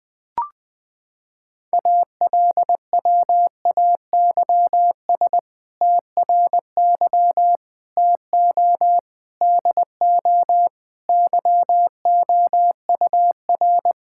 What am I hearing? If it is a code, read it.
Morse code, heard as a keyed tone, about 20 wpm, 700 Hz. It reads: ALWAYS TRY TO DO YOUR
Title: Sentences with 2-Word Phrases - 20wpm